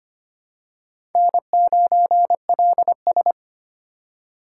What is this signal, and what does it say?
Morse code, heard as a keyed tone, about 25 wpm, 700 Hz. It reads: N9LS
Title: USA Call Signs - 4 Characters - 4x Character Spacing - 25wpm